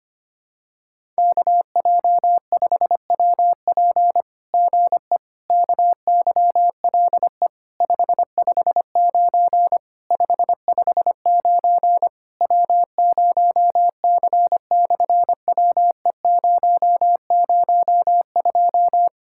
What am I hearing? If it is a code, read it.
Morse code, heard as a keyed tone, about 25 wpm, 700 Hz. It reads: KJ5WP GE KYLE 559 559 W0C/WE002
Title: SOTA QSO Practice 25wpm